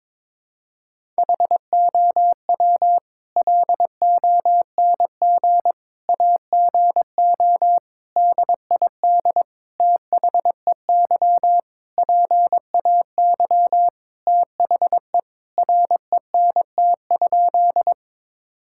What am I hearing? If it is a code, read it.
Morse code, heard as a keyed tone, about 22 wpm, 700 Hz. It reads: HOW LONG AGO DID THEY PAY THE RENT?
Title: Sentences from Top 1000 Words - Review Part 5 - 22wpm